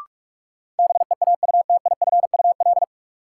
Morse code, heard as a keyed tone, about 46 wpm, 700 Hz. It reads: BEAUTIFUL
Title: Sentences from Top 900 Words - New Word Drill - 45wpm